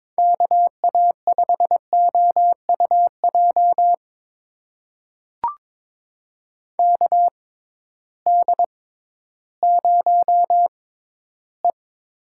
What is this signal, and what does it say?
Morse code, heard as a keyed tone, about 21 wpm, spaced as for 13 wpm, 700 Hz. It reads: KA5OUJ KD0E
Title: USA Call Signs - 6x Character Spacing - 22wpm